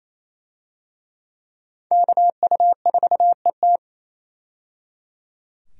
Morse code, heard as a keyed tone, about 27 wpm, 700 Hz. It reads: KU4ET
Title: USA Call Signs - 5 Characters - 28wpm